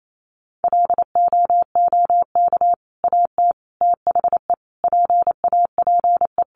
Morse code, heard as a keyed tone, about 28 wpm, 700 Hz. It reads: LOOK AT THE PAPE
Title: Sentences from Top 400 Words 28wpm